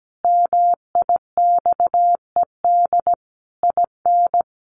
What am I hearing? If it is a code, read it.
Morse code, heard as a keyed tone, about 17 wpm, 700 Hz. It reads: MIXED IN
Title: Sentences from Top 1500 Words 17wpm